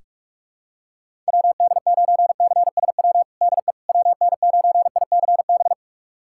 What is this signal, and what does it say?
Morse code, heard as a keyed tone, about 47 wpm, 700 Hz. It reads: WD9CSW DE WN9ICB
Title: USA Callsign DE Callsign 45wpm